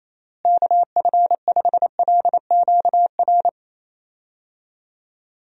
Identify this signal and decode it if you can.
Morse code, heard as a keyed tone, about 27 wpm, 700 Hz. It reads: KF5LQR
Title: USA Call Signs - 6 Characters - Repeated 3x - 28wpm